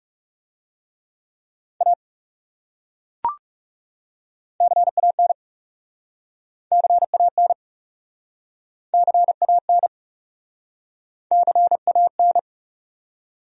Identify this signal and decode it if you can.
Morse code, heard as a keyed tone, about 40 wpm, 700 Hz. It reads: A CAN CAN CAN CAN